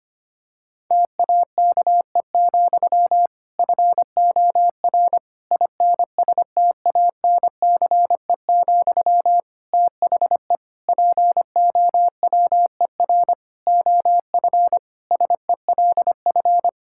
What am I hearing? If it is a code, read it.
Morse code, heard as a keyed tone, about 25 wpm, 700 Hz. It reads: TAKE, FOR INSTANCE, THE POWER OF SELF
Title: Sentences from Top 1000 Words - Review Part 5 - 25wpm